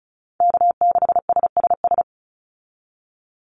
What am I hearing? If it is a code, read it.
Morse code, heard as a keyed tone, about 34 wpm, 700 Hz. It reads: K6SSS
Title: Call Signs - 35wpm